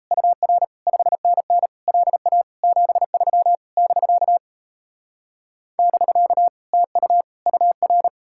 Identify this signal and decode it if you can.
Morse code, heard as a keyed tone, about 37 wpm, 700 Hz. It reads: UR 5NN LA 73 <BK> <BK> TU UR